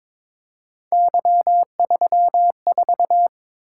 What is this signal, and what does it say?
Morse code, heard as a keyed tone, about 22 wpm, 700 Hz. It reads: Y34
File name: practice / 22wpm / Three Letter-Number Combinations - 22wpm